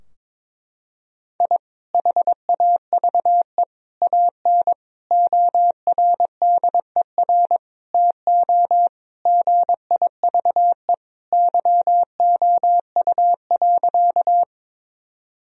Morse code, heard as a keyed tone, about 21 wpm, 700 Hz. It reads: I HAVE AN ORDER TO GIVE YOU.
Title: Sentences from Top 1000 Words - Review Part 2 - 22wpm